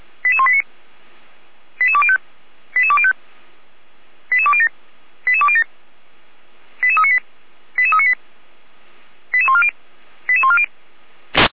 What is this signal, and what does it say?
Five-tone selective call (ZVEI), heard as quick runs of five tones, tones of about 70 ms. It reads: 88189 88205 88205 88207 88207 88398 88398 88140 88140
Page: /455/SDR/ZVEI/88179